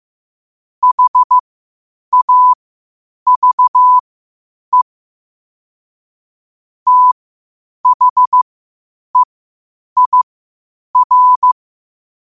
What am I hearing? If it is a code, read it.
Morse code, heard as a keyed tone, about 15 wpm, spaced as for 8 wpm, 1000 Hz. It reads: HAVE THEIR
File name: qotdfiles / 8wpm